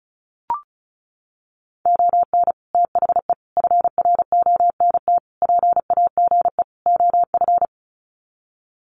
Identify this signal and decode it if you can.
Morse code, heard as a keyed tone, about 34 wpm, 700 Hz. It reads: ON THE FRONT PAGE OF
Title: Sets of 5 Words Encore 35wpm